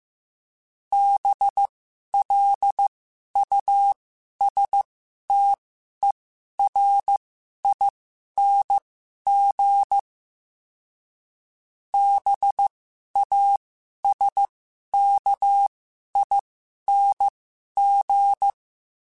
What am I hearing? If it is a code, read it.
Morse code, heard as a keyed tone, about 15 wpm, spaced as for 11 wpm, 800 Hz. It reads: BLUSTERING BASKING
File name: Special INGs-mp3s / Special INGs-mp3s / Sngl Ing @10